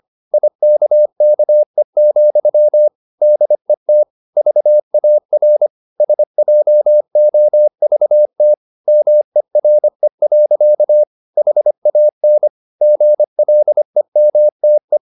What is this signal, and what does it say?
Morse code, heard as a keyed tone, about 25 wpm, 600 Hz. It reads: IKKE, DET VAR SJOVT MERE. HAN GLEMTE
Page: Ferie 25 wpm